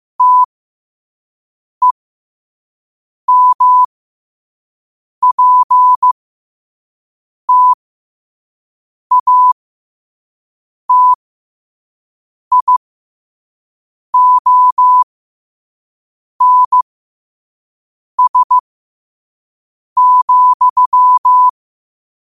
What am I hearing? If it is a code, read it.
Morse code, heard as a keyed tone, about 15 wpm, spaced as for 5 wpm, 1000 Hz. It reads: TEMPTATIONS,
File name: qotdfiles / 5wpm